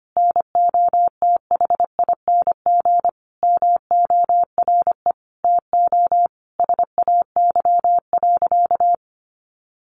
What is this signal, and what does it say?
Morse code, heard as a keyed tone, about 25 wpm, 700 Hz. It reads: NOTHING MORE TO SAY.
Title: Sentences from Top 500 Words 25wpm